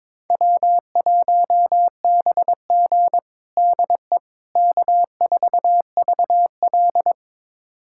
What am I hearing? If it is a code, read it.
Morse code, heard as a keyed tone, about 22 wpm, 700 Hz. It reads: W1BG DE K4VL
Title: USA Callsign DE Callsign - 4 Characters - 22wpm